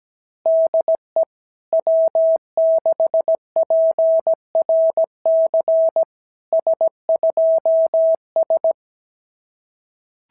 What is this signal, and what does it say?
Morse code, heard as a keyed tone, about 17 wpm, 650 Hz. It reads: DE W6PRC S2S
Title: SOTA QSO Practice 17wpm